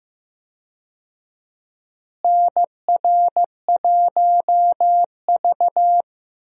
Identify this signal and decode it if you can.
Morse code, heard as a keyed tone, about 15 wpm, 700 Hz. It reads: NR1V
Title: USA Call Signs - 4 Characters - Repeated 2x - 15wpm